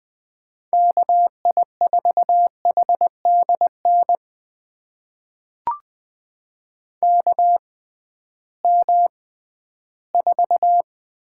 Morse code, heard as a keyed tone, about 20 wpm, spaced as for 13 wpm, 700 Hz. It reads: KI4HDN KM4
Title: USA Call Signs - 6 Characters - 6x Character Spacing - 20wpm